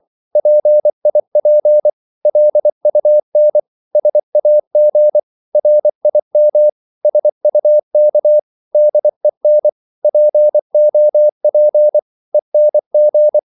Morse code, heard as a keyed tone, about 24 wpm, 600 Hz. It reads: PIP LUN SAG RIM SUK DEN POP ENG
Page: Korte danske ord 24wpm